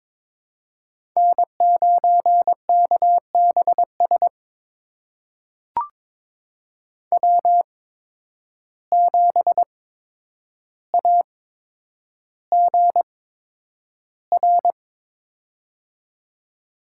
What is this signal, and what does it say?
Morse code, heard as a keyed tone, about 22 wpm, spaced as for 9 wpm, 700 Hz. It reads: N9KBS W7AGR
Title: USA Call Signs - 5 Characters - 8x Character Spacing - 22wpm